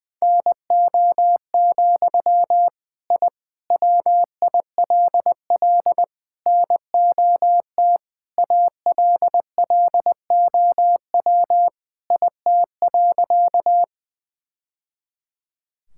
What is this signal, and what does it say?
Morse code, heard as a keyed tone, about 20 wpm, 700 Hz. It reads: NO, I WILL NOT ALLOW IT.
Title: Sentences from Top 1000 Words - Review Part 5 - 20wpm